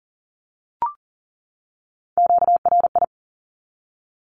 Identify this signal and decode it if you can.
Morse code, heard as a keyed tone, about 40 wpm, 700 Hz. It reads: QRI
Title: Q-Codes 40wpm